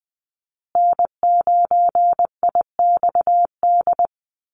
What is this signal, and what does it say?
Morse code, heard as a keyed tone, about 20 wpm, 700 Hz. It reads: N9IXD
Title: Call Signs Repeated 2x - 20wpm